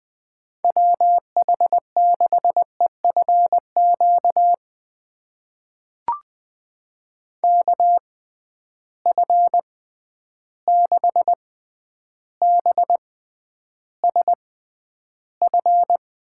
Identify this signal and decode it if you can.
Morse code, heard as a keyed tone, about 20 wpm, spaced as for 10 wpm, 700 Hz. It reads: WH6EFQ KF6BSF
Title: USA Call Signs - 6 Characters - 6x Character Spacing - 20wpm